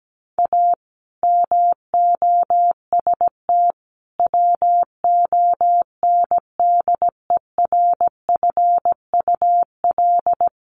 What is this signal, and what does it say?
Morse code, heard as a keyed tone, about 17 wpm, 700 Hz. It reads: A MOST WONDERFUL
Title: Sentences from Top 1100 Words 17wpm